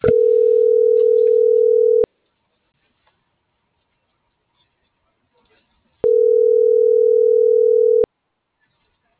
Ringback tone (North America), 440 + 480 Hz.